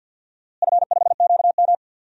Morse code, heard as a keyed tone, about 50 wpm, 700 Hz. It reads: F5QM
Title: DXCC-100 Call Signs - 4 Characters - Repeated 3x - 50wpm